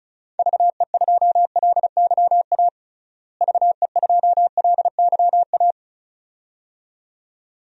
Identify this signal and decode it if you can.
Morse code, heard as a keyed tone, about 36 wpm, 700 Hz. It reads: VE2LYA VE2LYA
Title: DXCC-100 Call Signs - 6 Characters - Repeated 2x - 35wpm